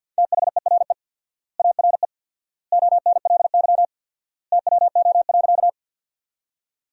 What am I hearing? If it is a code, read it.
Morse code, heard as a keyed tone, about 50 wpm, 700 Hz. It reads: THERE ARE ONLY TWO.
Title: 4x Word-Spacing - Sentences from Top 200 Words - 50wpm